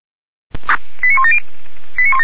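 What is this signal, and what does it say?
Five-tone selective call (ZVEI), heard as quick runs of five tones, tones of about 80 ms. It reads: 89170
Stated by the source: ZVEI five-tone selective call received on an SDR.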